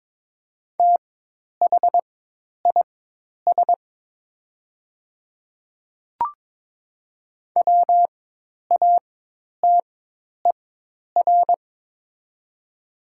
Morse code, heard as a keyed tone, about 22 wpm, spaced as for 10 wpm, 700 Hz. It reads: THIS WATER
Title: 4x Character Spacing - Top 100 Words - 22wpm